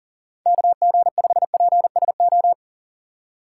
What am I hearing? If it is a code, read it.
Morse code, heard as a keyed tone, about 40 wpm, 700 Hz. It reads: KG5PSO
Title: USA Call Signs - 6 Characters - 6x Character Spacing - 40wpm